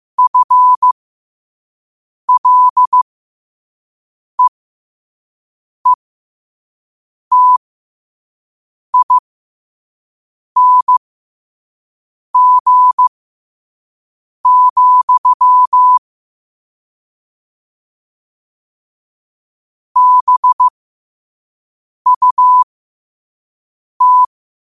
Morse code, heard as a keyed tone, about 15 wpm, spaced as for 5 wpm, 1000 Hz. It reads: FLEETING, BUT
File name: qotdfiles / 5wpm